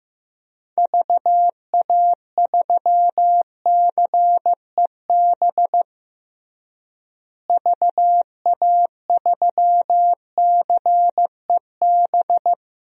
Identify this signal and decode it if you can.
Morse code, heard as a keyed tone, about 15 wpm, 700 Hz. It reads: VA3CEB VA3CEB
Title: DXCC-100 Call Signs - 6 Characters - Repeated 2x - 15wpm